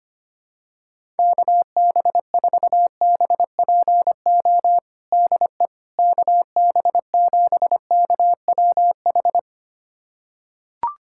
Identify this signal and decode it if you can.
Morse code, heard as a keyed tone, about 25 wpm, 700 Hz. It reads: KB4BPO DE KB7KWH
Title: USA Callsign DE Callsign - 6 Characters - 25wpm